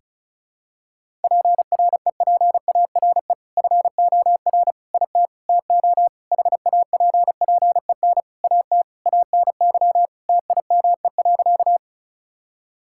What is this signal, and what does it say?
Morse code, heard as a keyed tone, about 36 wpm, 700 Hz. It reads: PREPARE FOR IT TO HAPPEN AT ANY TIME.
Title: Sentences from Top 1000 Words - Review Part 6 - 35wpm